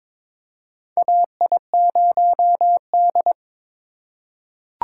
Morse code, heard as a keyed tone, about 22 wpm, 700 Hz. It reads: AI0D
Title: USA Call Signs - 4 Characters - 3x Character Spacing - 22wpm